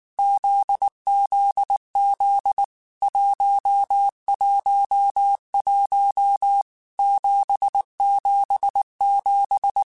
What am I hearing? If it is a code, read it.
Morse code, heard as a keyed tone, about 19 wpm, 800 Hz. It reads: ZZZ 111 777
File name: Basic Sets / Basic Letter Groups Collective7 / JZ17@18